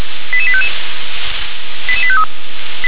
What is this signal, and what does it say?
Five-tone selective call (ZVEI), heard as quick runs of five tones, tones of about 70 ms. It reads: 88855 88853